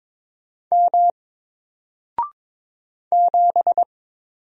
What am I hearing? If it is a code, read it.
Morse code, heard as a keyed tone, about 22 wpm, 700 Hz. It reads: M 7
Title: Single Letter-Number 22wpm